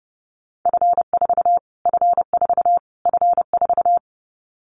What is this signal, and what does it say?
Morse code, heard as a keyed tone, about 30 wpm, 690 Hz. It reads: F4 F4 F4
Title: Callsign Prefixes for Top 25 DXCC - Repeated 3x - 30wpm